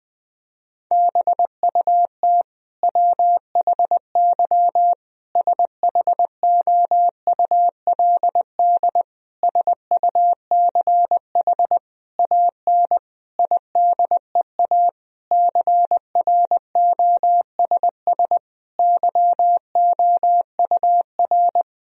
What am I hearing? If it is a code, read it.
Morse code, heard as a keyed tone, about 21 wpm, 700 Hz. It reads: BUT WHY SHOULD SUCH AN IDEA CROSS YOUR
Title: Sentences from Top 1000 Words - Review Part 1 - 20wpm